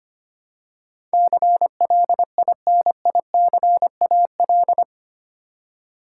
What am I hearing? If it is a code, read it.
Morse code, heard as a keyed tone, about 25 wpm, 700 Hz. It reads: CLINICAL
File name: Eight Letter Words - 25wpm